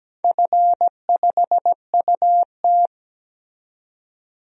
Morse code, heard as a keyed tone, about 17 wpm, 700 Hz. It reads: F5UT
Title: DXCC-100 Call Signs - 4 Characters - 17wpm